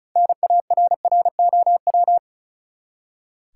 Morse code, heard as a keyed tone, about 36 wpm, 700 Hz. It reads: NARROW
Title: Six Letter Words - 35wpm